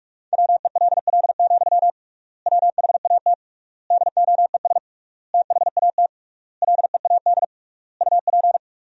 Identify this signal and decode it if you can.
Morse code, heard as a keyed tone, about 46 wpm, 700 Hz. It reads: WELL, WHAT DOES THAT LEAD UP
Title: 3x Word-Spacing - Sentences from Top 500 Words - 45wpm